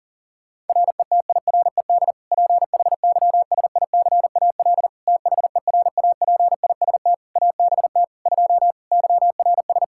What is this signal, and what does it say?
Morse code, heard as a keyed tone, about 40 wpm, 700 Hz. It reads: RETIRED PHYSICAL THERAPIST ABT 2 YRS
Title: K1USN SST QSO Practice 40wpm